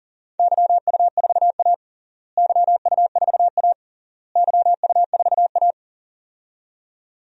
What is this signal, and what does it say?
Morse code, heard as a keyed tone, about 40 wpm, 700 Hz. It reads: YU4A YU4A YU4A
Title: DXCC-100 Call Signs - 4 Characters - Repeated 3x - 40wpm